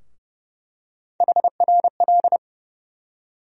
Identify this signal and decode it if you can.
Morse code, heard as a keyed tone, about 29 wpm, 700 Hz. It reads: HRL
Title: USA Call Sign Postfixes - 3 Letters 30wpm